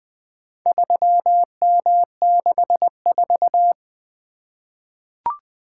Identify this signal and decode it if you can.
Morse code, heard as a keyed tone, about 20 wpm, 700 Hz. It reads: 3M64
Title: Four Letter-Number Combinations - 20wpm